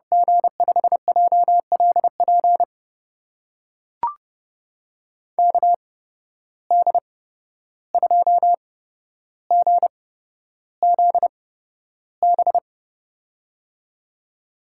Morse code, heard as a keyed tone, about 30 wpm, spaced as for 12 wpm, 700 Hz. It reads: G5JLP KD2GZB
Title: USA Call Signs - 6 Characters - 8x Character Spacing - 30wpm